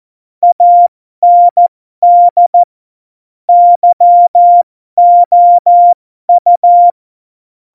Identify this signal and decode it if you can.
Morse code, heard as a keyed tone, about 14 wpm, spaced as for 12 wpm, 700 Hz. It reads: AND YOU
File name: Morsekurs / Witze